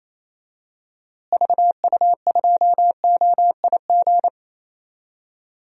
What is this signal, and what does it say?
Morse code, heard as a keyed tone, about 28 wpm, 700 Hz. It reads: VU2OIG
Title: DXCC-100 Call Signs - 6 Characters - Repeated 2x - 28wpm